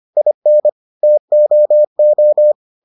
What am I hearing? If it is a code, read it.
Morse code, heard as a keyed tone, about 25 wpm, 600 Hz. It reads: IN TOO